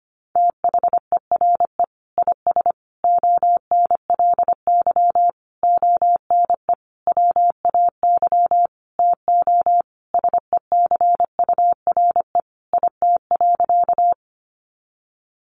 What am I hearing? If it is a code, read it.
Morse code, heard as a keyed tone, about 25 wpm, 700 Hz. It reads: THERE IS ONLY ONE WAY TO SECURE IT.
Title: Sentences from Top 1200 Words 25wpm